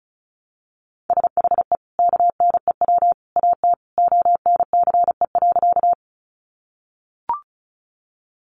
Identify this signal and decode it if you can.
Morse code, heard as a keyed tone, about 34 wpm, 700 Hz. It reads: SHE KNEW AT ONCE.
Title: Sentences from Top 400 Words 35wpm